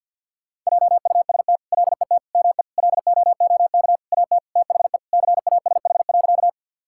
Morse code, heard as a keyed tone, about 50 wpm, 700 Hz. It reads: JUST LET ME LOOK AT THE CASH.
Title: Sentences from Top 1000 Words - Review Part 5 - 50wpm